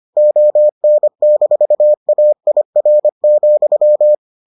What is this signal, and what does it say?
Morse code, heard as a keyed tone, about 25 wpm, 600 Hz. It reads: ON-AIR,